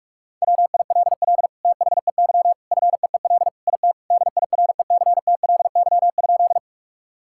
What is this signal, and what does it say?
Morse code, heard as a keyed tone, about 46 wpm, 700 Hz. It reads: WILL THEY FEEL IT DIRECTLY?
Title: Sentences from Top 1000 Words - Review Part 5 - 45wpm